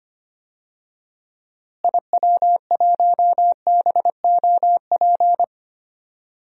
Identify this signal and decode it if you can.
Morse code, heard as a keyed tone, about 25 wpm, 700 Hz. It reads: IW1BOP
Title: DXCC-100 Call Signs - 6 Characters - Repeated 2x - 25wpm